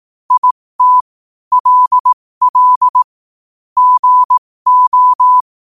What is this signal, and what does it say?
Morse code, heard as a keyed tone, about 18 wpm, spaced as for 15 wpm, 1000 Hz. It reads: IT LL GO
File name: qotdfiles / 15wpm